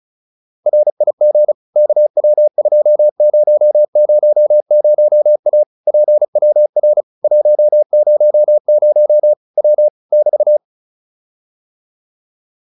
Morse code, heard as a keyed tone, about 34 wpm, 600 Hz. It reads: RIG KW2000A PWR 100 W <BT>
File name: QSO410 35